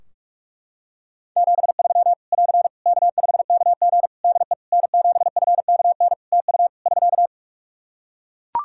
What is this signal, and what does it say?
Morse code, heard as a keyed tone, about 43 wpm, 700 Hz. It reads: 73 <AR> K5KG DE N7FKN TU <SK>
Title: K1USN SST QSO Practice 45wpm